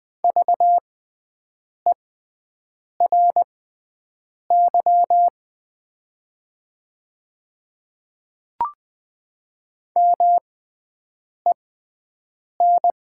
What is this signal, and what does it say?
Morse code, heard as a keyed tone, about 20 wpm, spaced as for 7 wpm, 700 Hz. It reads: VERY MEN
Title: 6x Character Spacing - Top 300 Words - 20wpm